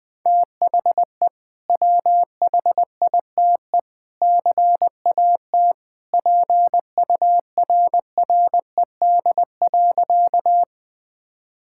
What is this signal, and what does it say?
Morse code, heard as a keyed tone, about 19 wpm, 700 Hz. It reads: THE WHITE CAT PURRED.
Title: Sentences with 2-Word Phrases - Spoken Beginning - 20wpm